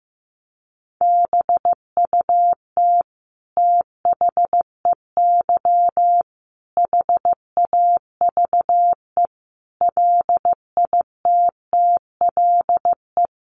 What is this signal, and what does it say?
Morse code, heard as a keyed tone, about 15 wpm, 700 Hz. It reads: BUT THEY HAVE LITTLE
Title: Sentences from Top 1200 Words 15wpm